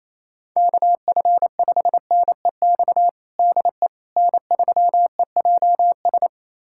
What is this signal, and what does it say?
Morse code, heard as a keyed tone, about 28 wpm, 700 Hz. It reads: KF5NEX DE N3EJS
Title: USA Callsign DE Callsign 28wpm